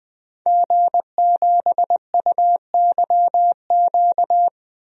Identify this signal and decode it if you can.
Morse code, heard as a keyed tone, about 20 wpm, 700 Hz. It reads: G7UYQ
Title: DXCC-100 Call Signs - 5 Characters - 20wpm